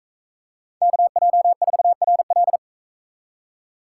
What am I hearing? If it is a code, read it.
Morse code, heard as a keyed tone, about 44 wpm, 700 Hz. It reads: KJ4RL